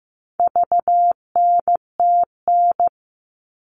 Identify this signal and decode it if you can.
Morse code, heard as a keyed tone, about 15 wpm, 700 Hz. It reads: VNTN
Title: All Cut Numbers - 4-Digit 15wpm